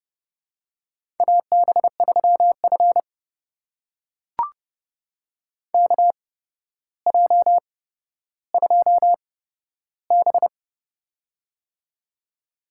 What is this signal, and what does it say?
Morse code, heard as a keyed tone, about 30 wpm, spaced as for 13 wpm, 700 Hz. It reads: AB3F KJ2B
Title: USA Call Signs - 4 Characters - 8x Character Spacing - 30wpm